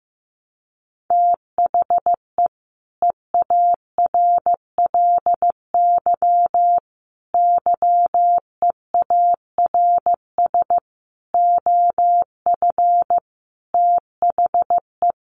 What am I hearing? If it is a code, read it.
Morse code, heard as a keyed tone, about 15 wpm, 700 Hz. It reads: THE EARLY YEARS OF THE
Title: Sets of 5 Words Encore 15wpm